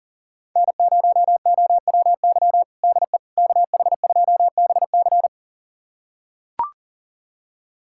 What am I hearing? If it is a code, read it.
Morse code, heard as a keyed tone, about 40 wpm, 700 Hz. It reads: N0OWY DE KH2BC
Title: USA Callsign DE Callsign - 5 Characters - 40wpm